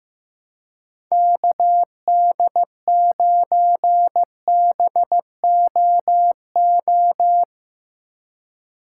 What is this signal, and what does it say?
Morse code, heard as a keyed tone, about 15 wpm, 700 Hz. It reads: KD9BOO
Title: USA Call Signs - 2x Character Spacing - 15wpm